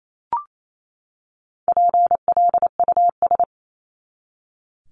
Morse code, heard as a keyed tone, about 28 wpm, 700 Hz. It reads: PLUS
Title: Top 1001-1250 Words 28wpm